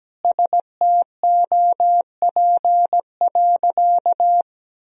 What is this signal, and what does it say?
Morse code, heard as a keyed tone, about 17 wpm, 700 Hz. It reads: STOP.
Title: Sentences from Top 1000 Words - Review Part 2 - 17wpm